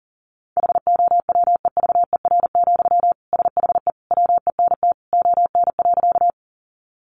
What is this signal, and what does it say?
Morse code, heard as a keyed tone, about 37 wpm, 700 Hz. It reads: HOWEVER, SHE WENT ON.
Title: Sentences from Top 600 Words 40wpm